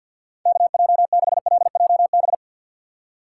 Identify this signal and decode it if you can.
Morse code, heard as a keyed tone, about 52 wpm, 700 Hz. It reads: KJ6LJB
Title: USA Call Signs - 6 Characters - 3x Character Spacing - 50wpm